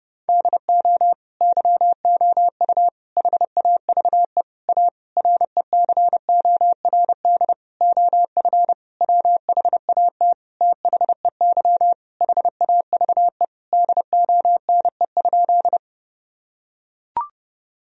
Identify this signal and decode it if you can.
Morse code, heard as a keyed tone, about 30 wpm, 700 Hz. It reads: DO YOU HAVE A RECORD OF WHAT THEY HAVE DONE?
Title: Sentences from Top 1000 Words - Review Part 3 - 30wpm